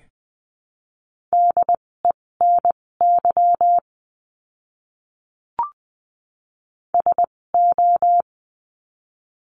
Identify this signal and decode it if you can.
Morse code, heard as a keyed tone, about 20 wpm, spaced as for 16 wpm, 700 Hz. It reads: DENY SO
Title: 256 Lesson 42a - Words 12-20wpm